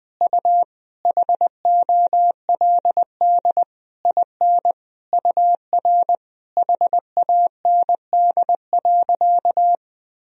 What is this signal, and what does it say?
Morse code, heard as a keyed tone, about 20 wpm, 700 Hz. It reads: U HOLD IN UR HAND.
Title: Sentences from Top 400 Words and Abbreviations 20wpm